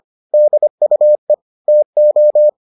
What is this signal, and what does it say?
Morse code, heard as a keyed tone, about 25 wpm, 600 Hz. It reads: DUE TO